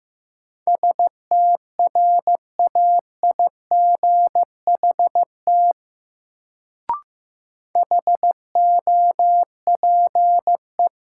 Morse code, heard as a keyed tone, about 15 wpm, 700 Hz. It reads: STRAIGHT HOPE
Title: Sentences from Top 1000 Words - New Word Drill - 15wpm